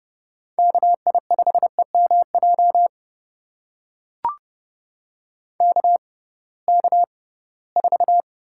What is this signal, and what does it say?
Morse code, heard as a keyed tone, about 32 wpm, spaced as for 20 wpm, 700 Hz. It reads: KI5EMJ KK4
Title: USA Call Signs - 6 Characters - 6x Character Spacing - 30wpm